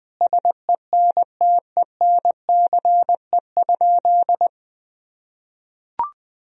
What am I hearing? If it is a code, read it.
Morse code, heard as a keyed tone, about 20 wpm, 700 Hz. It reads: SENTENCE?
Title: Sentences from Top 1000 Words - Review Part 1 - 20wpm